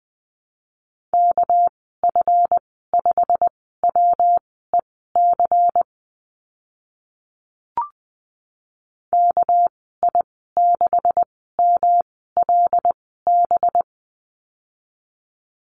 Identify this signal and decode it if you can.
Morse code, heard as a keyed tone, about 20 wpm, spaced as for 14 wpm, 700 Hz. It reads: KF5WEC KI6MLB
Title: 219 Lesson 35g - Number 0 - Call Signs 20wpm